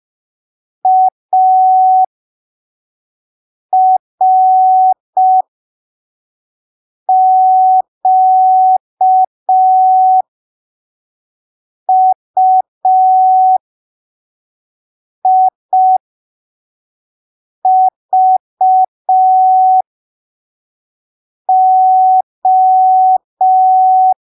Morse code, heard as a keyed tone, about 5 wpm, spaced as for 3 wpm, 750 Hz. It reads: ARQUIVO
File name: arquivos / texto / texto 02 5ppm